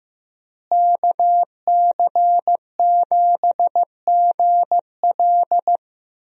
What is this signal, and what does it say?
Morse code, heard as a keyed tone, about 15 wpm, 700 Hz. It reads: KC7GL
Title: USA Call Signs - 5 Characters - Repeated 2x - 15wpm